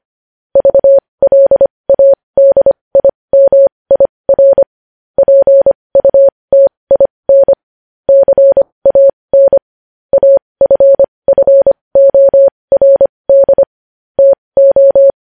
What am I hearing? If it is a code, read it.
Morse code, heard as a keyed tone, about 25 wpm, spaced as for 20 wpm, 550 Hz. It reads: VLADIMIR PUTIN CAN AFFORD TO
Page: News Headlines in Morse Code at 20 WPM